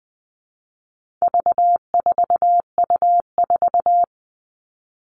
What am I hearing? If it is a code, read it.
Morse code, heard as a keyed tone, about 20 wpm, 700 Hz. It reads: V4U4
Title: All Cut Numbers - 4-Digit 20wpm